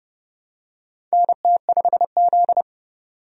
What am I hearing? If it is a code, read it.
Morse code, heard as a keyed tone, about 31 wpm, 700 Hz. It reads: NT5Z
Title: USA Call Signs - 4 Characters - 30wpm